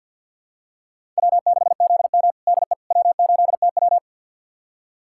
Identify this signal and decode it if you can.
Morse code, heard as a keyed tone, about 52 wpm, 700 Hz. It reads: W6ZM DE W8TW
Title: USA Callsign DE Callsign - 4 Characters - 50wpm